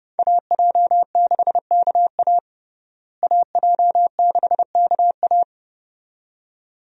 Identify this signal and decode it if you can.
Morse code, heard as a keyed tone, about 32 wpm, 700 Hz. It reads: AJ6KA AJ6KA
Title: USA Call Signs - 5 Characters - Repeated 2x - 30wpm